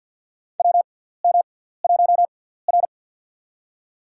Morse code, heard as a keyed tone, about 53 wpm, spaced as for 17 wpm, 700 Hz. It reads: WM1R
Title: USA Call Signs - 4 Characters - 6x Character Spacing - 50wpm